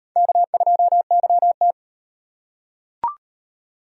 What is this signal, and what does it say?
Morse code, heard as a keyed tone, about 38 wpm, 700 Hz. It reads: K2YT